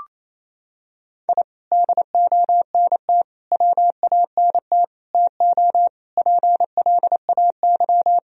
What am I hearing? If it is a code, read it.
Morse code, heard as a keyed tone, about 27 wpm, 700 Hz. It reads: I DONT WANT TO PLAY
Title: Sentences from Top 1000 Words - Review Part 2 - 28wpm